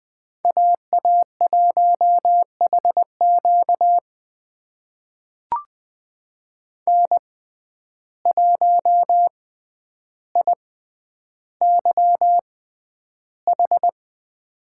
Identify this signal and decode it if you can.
Morse code, heard as a keyed tone, about 20 wpm, spaced as for 10 wpm, 700 Hz. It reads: AA1HQ N1IYH
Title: USA Call Signs - 5 Characters - 6x Character Spacing - 20wpm